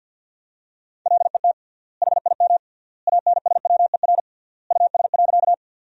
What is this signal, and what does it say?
Morse code, heard as a keyed tone, about 51 wpm, 700 Hz. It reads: LET HIM ANSWER US.
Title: 3x Word-Spacing - Sentences from Top 300 Words - 50wpm